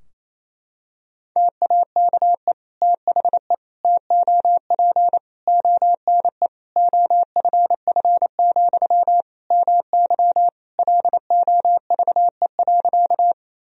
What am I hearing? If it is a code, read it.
Morse code, heard as a keyed tone, about 27 wpm, 700 Hz. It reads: TAKE THE TOP ONE OFF, MY LOVE.
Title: Sentences from Top 1000 Words - Review Part 2 - 28wpm